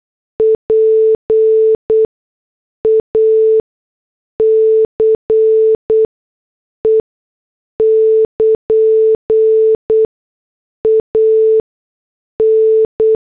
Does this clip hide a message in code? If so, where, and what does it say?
Yes: Morse code from 0.4 to 13.2 s, heard as a keyed tone, about 8 wpm, spaced as for 6 wpm, 440 Hz. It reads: PACE <KN> AN